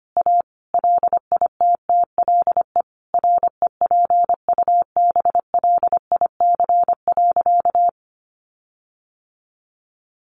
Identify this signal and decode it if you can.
Morse code, heard as a keyed tone, about 25 wpm, 700 Hz. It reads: A LITTLE REPUBLIC.
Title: Sentences from Top 1500 Words 25wpm